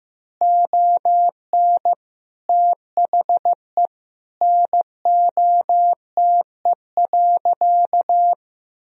Morse code, heard as a keyed tone, about 16 wpm, 700 Hz. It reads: ON THE NOTE.
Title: Sentences from Top 1000 Words - Review Part 2 - 15wpm